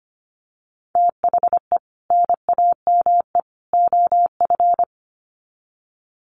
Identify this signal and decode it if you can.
Morse code, heard as a keyed tone, about 25 wpm, 700 Hz. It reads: THE NAME OF
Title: Sets of 3 Words 25wpm